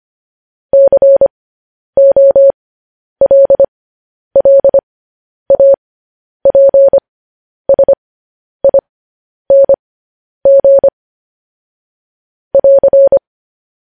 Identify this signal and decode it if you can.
Morse code, heard as a keyed tone, about 25 wpm, spaced as for 10 wpm, 550 Hz. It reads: COLLAPSING <AR>
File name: podcast / News headlines-1x-25x10